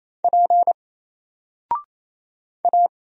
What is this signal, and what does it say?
Morse code, heard as a keyed tone, about 27 wpm, 700 Hz. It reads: P A
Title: Single Character (Letter:Number:Punctuation:Prosign) 28wpm